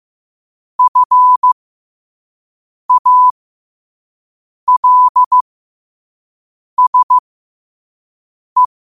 Morse code, heard as a keyed tone, about 15 wpm, spaced as for 5 wpm, 1000 Hz. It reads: FALSE
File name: 5wpm